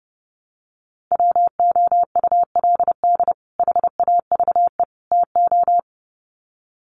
Morse code, heard as a keyed tone, about 30 wpm, 700 Hz. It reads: WOULD HAVE TO
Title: Sets of 3 Words 30wpm